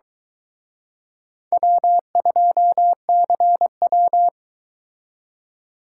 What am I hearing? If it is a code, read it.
Morse code, heard as a keyed tone, about 22 wpm, 700 Hz. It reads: W2CW